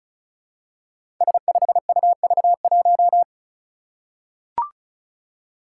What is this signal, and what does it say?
Morse code, heard as a keyed tone, about 36 wpm, 700 Hz. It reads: S5UV1
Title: Five Challenging Letter-Number Combinations 35wpm